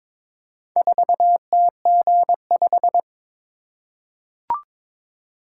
Morse code, heard as a keyed tone, about 22 wpm, 700 Hz. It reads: 4TG5
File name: practice / 22wpm / Four Letter-Number Combinations - 22wpm